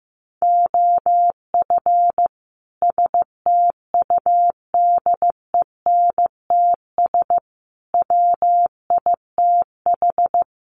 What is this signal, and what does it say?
Morse code, heard as a keyed tone, about 15 wpm, 700 Hz. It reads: OF STUDENTS WITH
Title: Sets of 5 Words Encore 15wpm